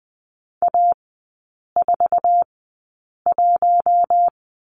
Morse code, heard as a keyed tone, about 20 wpm, spaced as for 8 wpm, 700 Hz. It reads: A41
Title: 060 Lesson 11e - Letter h - Three Characters 20wpm